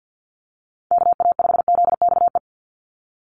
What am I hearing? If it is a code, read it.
Morse code, heard as a keyed tone, about 50 wpm, 700 Hz. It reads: KA5ZXE
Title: USA Call Signs 50wpm